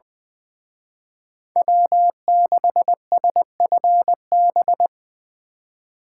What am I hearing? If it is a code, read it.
Morse code, heard as a keyed tone, about 23 wpm, 700 Hz. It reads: W6SFB
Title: Speed Racing - USA 1x3 Call Signs 20wpm